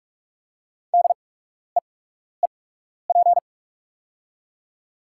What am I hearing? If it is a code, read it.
Morse code, heard as a keyed tone, about 48 wpm, spaced as for 12 wpm, 690 Hz. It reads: DEEP
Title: 8x Character Spacing - Top 500 Words - 45wpm